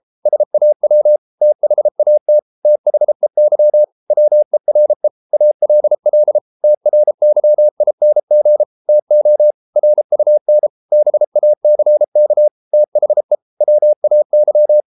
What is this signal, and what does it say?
Morse code, heard as a keyed tone, about 33 wpm, 600 Hz. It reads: SAW THAT THEY WERE ALL TRYING TO RUN BACK THE WAY